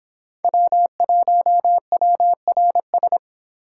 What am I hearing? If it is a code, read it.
Morse code, heard as a keyed tone, about 26 wpm, 700 Hz. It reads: W1WRS